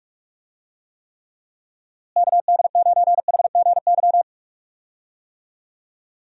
Morse code, heard as a keyed tone, about 45 wpm, 700 Hz. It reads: KD9HGY
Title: USA Call Signs - 6 Characters - Repeated 2x - 45wpm